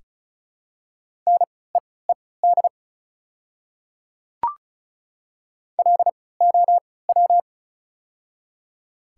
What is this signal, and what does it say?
Morse code, heard as a keyed tone, about 34 wpm, spaced as for 20 wpm, 700 Hz. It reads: NEED LOW
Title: 3x Character Spacing - Top 200 Words - 35wpm